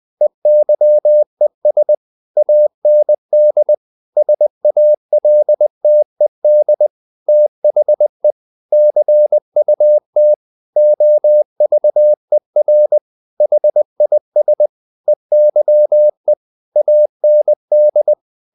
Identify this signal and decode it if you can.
Morse code, heard as a keyed tone, about 20 wpm, 600 Hz. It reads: EYES AND SALTED THE CUT OVER HIS EYE AND
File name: the old man and the sea part 2 20wpm